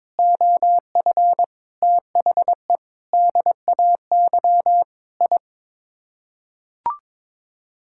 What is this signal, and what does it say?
Morse code, heard as a keyed tone, about 22 wpm, 700 Hz. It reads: OF THE DAY I
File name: practice / 22wpm / Sets of 8 Words - 22wpm